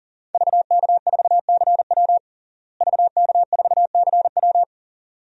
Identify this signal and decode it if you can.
Morse code, heard as a keyed tone, about 40 wpm, 690 Hz. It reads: VK4CW VK4CW
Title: DXCC-100 Call Signs - 5 Characters - Repeated 2x - 40wpm